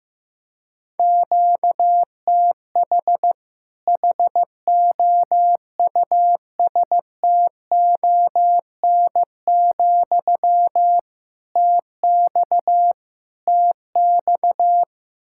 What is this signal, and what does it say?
Morse code, heard as a keyed tone, about 15 wpm, 700 Hz. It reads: QTH HOUSTON, TX TX
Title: Practice QSO - Exchange 1 - Beginning - 15wpm